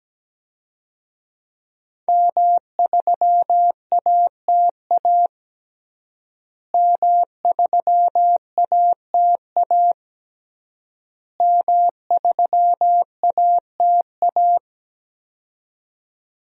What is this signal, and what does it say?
Morse code, heard as a keyed tone, about 17 wpm, 700 Hz. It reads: M3ATA M3ATA M3ATA
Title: DXCC-100 Call Signs - 5 Characters - Repeated 3x - 17wpm